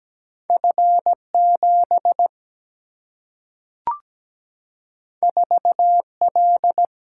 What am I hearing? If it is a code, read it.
Morse code, heard as a keyed tone, about 17 wpm, 700 Hz. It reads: F7 4L
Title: Single-Letter Single-Number 17wpm